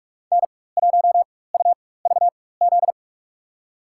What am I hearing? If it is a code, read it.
Morse code, heard as a keyed tone, about 43 wpm, spaced as for 21 wpm, 700 Hz. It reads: N1UVZ
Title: USA Call Signs - 4x Character Spacing - 45wpm